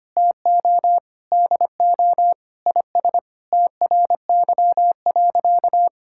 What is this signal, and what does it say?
Morse code, heard as a keyed tone, about 25 wpm, 700 Hz. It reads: TO DO IS TRY.
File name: practice / 25wpm / Sentences with 3-Word Phrases - 25wpm